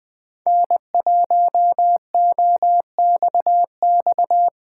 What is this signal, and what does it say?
Morse code, heard as a keyed tone, about 21 wpm, 700 Hz. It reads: N1OXX
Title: Speed Racing - USA 1x3 Call Signs 17wpm